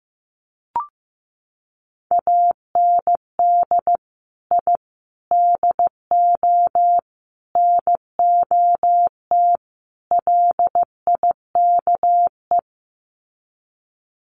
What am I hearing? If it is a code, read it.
Morse code, heard as a keyed tone, about 14 wpm, 700 Hz. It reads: AND I DO NOT LIKE
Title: Sets of 5 Words Encore2 15wpm